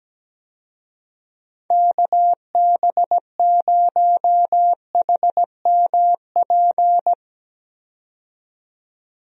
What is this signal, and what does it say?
Morse code, heard as a keyed tone, about 17 wpm, 700 Hz. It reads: KB0HMP
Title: USA Call Signs - 6 Characters - 17wpm